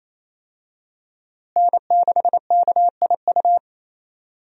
Morse code, heard as a keyed tone, about 28 wpm, 700 Hz. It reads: N6KIU
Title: USA Call Signs - 5 Characters - Repeated 3x - 28wpm